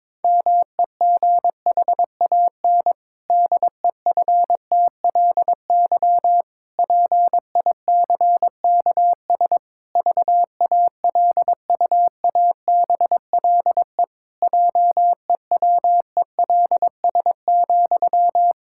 Morse code, heard as a keyed tone, about 22 wpm, 700 Hz. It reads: MEGHAN DEFTLY PICKS VALUABLE JEWELS,
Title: Pangram Challenge v2 - 22wpm